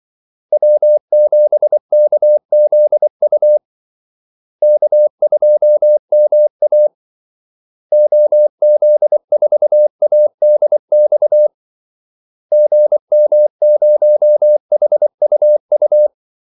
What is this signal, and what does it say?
Morse code, heard as a keyed tone, about 24 wpm, 600 Hz. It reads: W7KZU K2MA OZ4ADX GM0HUU
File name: Callsigns-24wpm